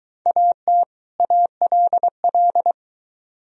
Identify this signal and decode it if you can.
Morse code, heard as a keyed tone, about 22 wpm, 700 Hz. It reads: AT ALL